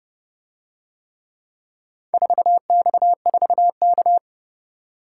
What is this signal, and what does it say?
Morse code, heard as a keyed tone, about 30 wpm, 700 Hz. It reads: 4X4K